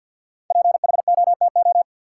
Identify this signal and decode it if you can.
Morse code, heard as a keyed tone, about 51 wpm, 700 Hz. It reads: PHOTO
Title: Sentences from Top 1200 Words - New Word Drill Alt - 50wpm